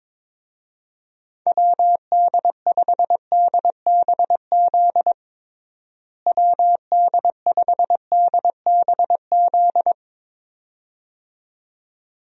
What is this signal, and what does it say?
Morse code, heard as a keyed tone, about 22 wpm, 700 Hz. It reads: WD5DBZ WD5DBZ
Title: USA Call Signs - 6 Characters - Repeated 2x - 22wpm